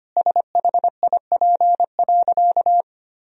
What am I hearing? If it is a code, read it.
Morse code, heard as a keyed tone, about 25 wpm, 700 Hz. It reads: SHIP.
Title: Sentences from Top 1000 Words - Review Part 2 - 25wpm